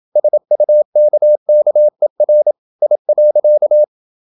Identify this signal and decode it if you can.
Morse code, heard as a keyed tone, about 28 wpm, 600 Hz. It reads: SUKKER I.
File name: Ferie QRQ 27wpm